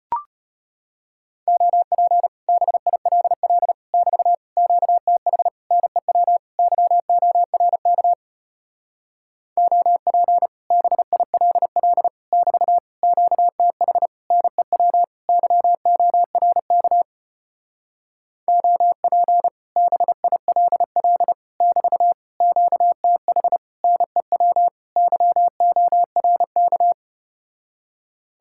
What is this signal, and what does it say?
Morse code, heard as a keyed tone, about 33 wpm, 700 Hz. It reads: OP BILL <BT> QTH NEW YORK OP BILL <BT> QTH NEW YORK OP BILL <BT> QTH NEW YORK